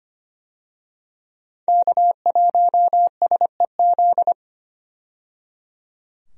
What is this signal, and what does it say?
Morse code, heard as a keyed tone, about 25 wpm, 700 Hz. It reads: K1SEZ
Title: USA Call Signs - 5 Characters - 25wpm